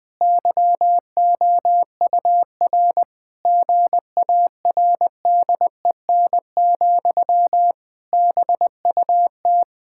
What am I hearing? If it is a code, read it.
Morse code, heard as a keyed tone, about 21 wpm, 700 Hz. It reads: YOUR GARDEN, BUT
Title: Pangram Challenge v2 - 20wpm